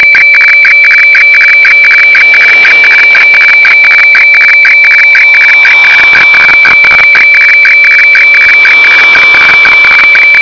Weather fax (radiofax).
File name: HF FAX